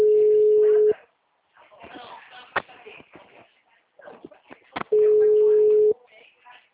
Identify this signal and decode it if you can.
Ringback tone (Europe, 425 Hz), with speech.
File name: out- -2013- - - .52230